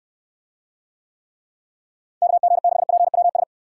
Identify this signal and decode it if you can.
Morse code, heard as a keyed tone, about 67 wpm, 700 Hz. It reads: KG6PWS